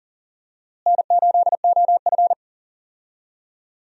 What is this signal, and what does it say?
Morse code, heard as a keyed tone, about 40 wpm, 700 Hz. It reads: N8OF
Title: USA Call Signs - 4 Characters - Repeated 3x - 40wpm